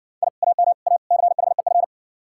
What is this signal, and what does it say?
Morse code, heard as a keyed tone, about 60 wpm, 700 Hz. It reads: I AM A CHEF